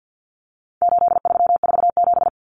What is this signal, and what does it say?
Morse code, heard as a keyed tone, about 50 wpm, 700 Hz. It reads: 8347
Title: Numbers - 4-Digit 50wpm